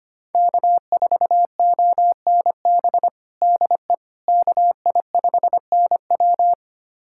Morse code, heard as a keyed tone, about 25 wpm, 700 Hz. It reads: K4ONB DE KI5NW
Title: USA Callsign DE Callsign 25wpm